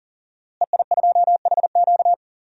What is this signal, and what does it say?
Morse code, heard as a keyed tone, about 40 wpm, 700 Hz. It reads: EI2HQ
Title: DXCC-100 Call Signs - 5 Characters - Repeated 3x - 40wpm